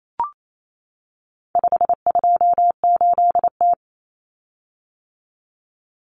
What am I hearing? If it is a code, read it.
Morse code, heard as a keyed tone, about 28 wpm, 700 Hz. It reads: 528T
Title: Common Cut Numbers - 4-Digit 28wpm